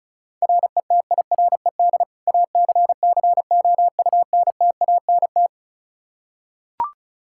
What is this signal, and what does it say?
Morse code, heard as a keyed tone, about 36 wpm, 700 Hz. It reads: RETIRED ACCOUNTANT
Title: Practice QSO - Exchange 3 - More Info - 35wpm